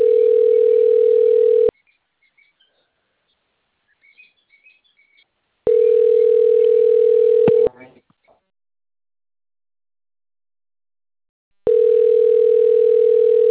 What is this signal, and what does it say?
Ringback tone (North America), 440 + 480 Hz.